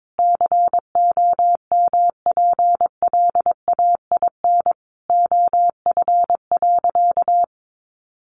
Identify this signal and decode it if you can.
Morse code, heard as a keyed tone, about 22 wpm, 700 Hz. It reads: COMPLAIN OF.
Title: Sentences from Top 1300 Words 22wpm